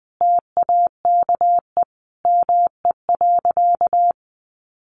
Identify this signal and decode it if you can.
Morse code, heard as a keyed tone, about 19 wpm, 700 Hz. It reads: TAKE ME.
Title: Sentences from Top 400 Words 20wpm